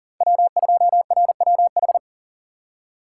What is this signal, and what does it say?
Morse code, heard as a keyed tone, about 37 wpm, 700 Hz. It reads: W2RWH
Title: USA Call Signs - 5 Characters - 3x Character Spacing - 40wpm